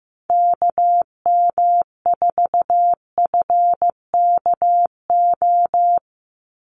Morse code, heard as a keyed tone, about 15 wpm, 700 Hz. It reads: KM4FKO
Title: USA Call Signs 15wpm